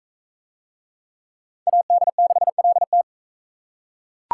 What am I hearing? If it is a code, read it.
Morse code, heard as a keyed tone, about 39 wpm, 700 Hz. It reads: AD6LT